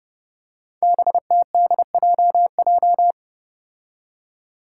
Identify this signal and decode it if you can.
Morse code, heard as a keyed tone, about 30 wpm, 700 Hz. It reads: BTDJJ
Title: Five Challenging Letter-Number Combinations 30wpm